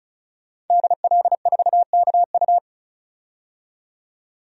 Morse code, heard as a keyed tone, about 34 wpm, 700 Hz. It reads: DL4KU
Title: DXCC-100 Call Signs - 5 Characters - 35wpm